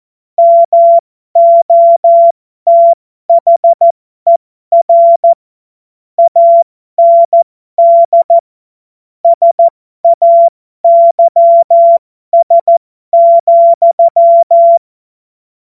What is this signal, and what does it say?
Morse code, heard as a keyed tone, about 14 wpm, spaced as for 12 wpm, 680 Hz. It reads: MOTHER AND SAYS,